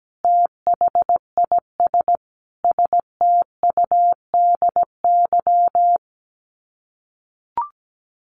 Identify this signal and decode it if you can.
Morse code, heard as a keyed tone, about 17 wpm, 700 Hz. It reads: THIS STUDY
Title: Sets of 5 Words 17wpm